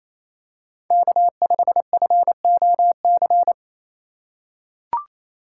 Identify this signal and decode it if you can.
Morse code, heard as a keyed tone, about 28 wpm, 700 Hz. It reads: K5FOC
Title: USA Call Signs - 5 Characters - 4x Character Spacing - 28wpm